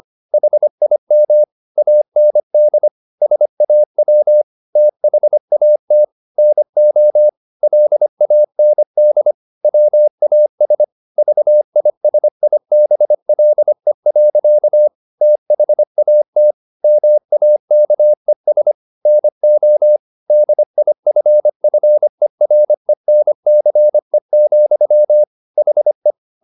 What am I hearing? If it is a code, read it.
Morse code, heard as a keyed tone, about 25 wpm, 600 Hz. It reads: HIM AND SAW THAT NO LAND WAS VISIBLE. THAT MAKES NO DIFFERENCE, HE
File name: the old man and the sea part 1 25wpm